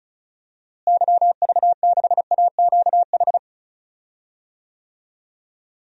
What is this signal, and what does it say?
Morse code, heard as a keyed tone, about 34 wpm, 700 Hz. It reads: YV6AQH
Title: DXCC-100 Call Signs - 6 Characters - Repeated 2x - 35wpm